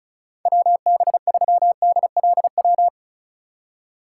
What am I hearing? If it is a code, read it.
Morse code, heard as a keyed tone, about 34 wpm, 700 Hz. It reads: WB3DLW
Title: USA Call Signs - 6 Characters - 2x Character Spacing - 35wpm